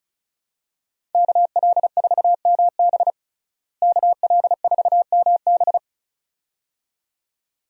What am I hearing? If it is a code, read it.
Morse code, heard as a keyed tone, about 36 wpm, 700 Hz. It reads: KL4MB KL4MB
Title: USA Call Signs - 5 Characters - Repeated 2x - 35wpm